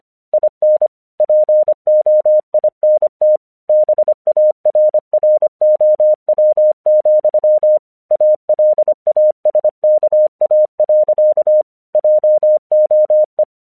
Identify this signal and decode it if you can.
Morse code, heard as a keyed tone, about 25 wpm, 610 Hz. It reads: IN POINT BARROW, ALASKA. JOE